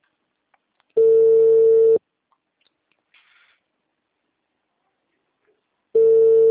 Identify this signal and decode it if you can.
Ringback tone (China, 450 Hz).